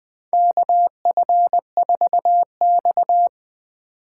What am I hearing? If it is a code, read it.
Morse code, heard as a keyed tone, about 20 wpm, 700 Hz. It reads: KF4X
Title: USA Call Signs - 4 Characters - 2x Character Spacing - 20wpm